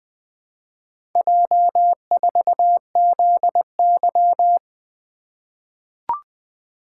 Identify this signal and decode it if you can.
Morse code, heard as a keyed tone, about 20 wpm, 700 Hz. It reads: J4ZY
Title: Four Letter-Number Combinations - 20wpm